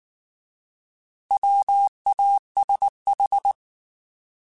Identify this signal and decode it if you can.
Morse code, heard as a keyed tone, about 19 wpm, 800 Hz. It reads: WASH
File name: Alphabetical / Alphabetical mp3s / Ws @18